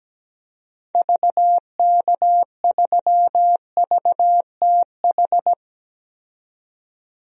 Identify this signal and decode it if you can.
Morse code, heard as a keyed tone, about 17 wpm, 700 Hz. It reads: VK3VTH
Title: DXCC-100 Call Signs - 6 Characters - 17wpm